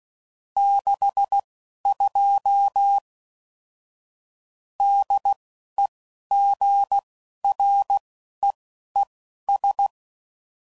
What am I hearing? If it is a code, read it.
Morse code, heard as a keyed tone, about 16 wpm, spaced as for 11 wpm, 800 Hz. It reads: 62 DEGREES
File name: CW Code Groups @10WPM / Weather Sets / Weather 4 28